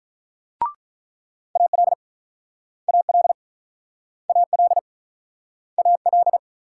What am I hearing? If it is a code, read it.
Morse code, heard as a keyed tone, about 46 wpm, 700 Hz. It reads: AL AL AL AL